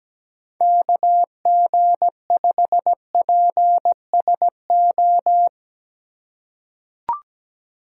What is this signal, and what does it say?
Morse code, heard as a keyed tone, about 17 wpm, 700 Hz. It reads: KG5PSO
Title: USA Call Signs - 6 Characters - 4x Character Spacing - 17wpm